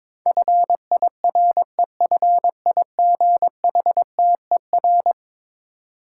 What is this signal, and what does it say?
Morse code, heard as a keyed tone, about 22 wpm, 700 Hz. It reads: FIREFIGHTER
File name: practice / 22wpm / Eleven Letter Words - 22wpm